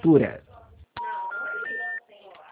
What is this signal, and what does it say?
Special information tone, with speech.